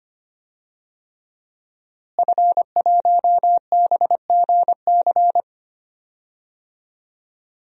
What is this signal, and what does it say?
Morse code, heard as a keyed tone, about 25 wpm, 700 Hz. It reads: F1BGC
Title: DXCC-100 Call Signs - 5 Characters - Repeated 3x - 25wpm